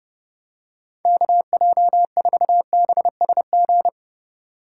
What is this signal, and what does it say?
Morse code, heard as a keyed tone, about 32 wpm, 700 Hz. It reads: KJ4BSG
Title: USA Call Signs - 6 Characters - 2x Character Spacing - 30wpm